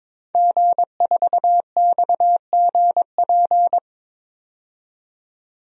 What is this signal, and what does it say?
Morse code, heard as a keyed tone, about 22 wpm, 700 Hz. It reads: G4XGP
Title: DXCC-100 Call Signs - 5 Characters - 22wpm